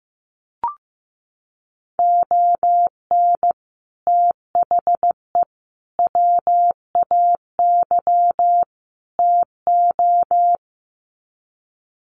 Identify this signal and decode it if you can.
Morse code, heard as a keyed tone, about 15 wpm, 700 Hz. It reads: ON THE WAY TO
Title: Sets of 4 Words 15wpm